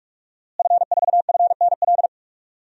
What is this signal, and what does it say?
Morse code, heard as a keyed tone, about 46 wpm, 700 Hz. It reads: F4FNL
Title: DXCC-100 Call Signs - 5 Characters - Repeated 3x - 45wpm